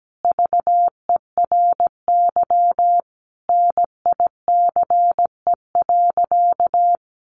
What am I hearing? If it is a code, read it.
Morse code, heard as a keyed tone, about 17 wpm, 700 Hz. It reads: VERY NICE.
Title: Sentences from Top 1100 Words 17wpm